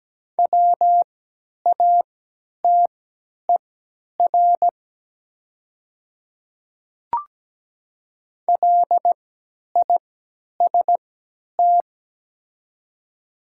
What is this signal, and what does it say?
Morse code, heard as a keyed tone, about 17 wpm, spaced as for 10 wpm, 700 Hz. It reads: WATER LIST
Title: 3x Character Spacing - Top 500 Words - 17wpm